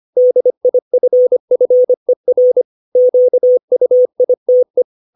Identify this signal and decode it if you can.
Morse code, heard as a keyed tone, about 25 wpm, 500 Hz. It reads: DIFFER QUITE
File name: 632 words / 632 Words 25WPM